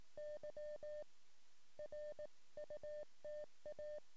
Morse code, heard as a keyed tone, about 18 wpm, 600 Hz. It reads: Y RUTA
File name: sm7lcb / 08 sm5le-p jo98ax 23cm